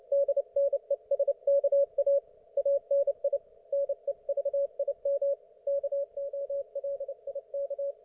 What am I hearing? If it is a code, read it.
Morse code, heard as a keyed tone, about 29 wpm, 560 Hz. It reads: DNESKA ANI NEVIM KOLIK